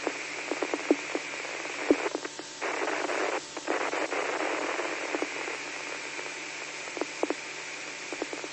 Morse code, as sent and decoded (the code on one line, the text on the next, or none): none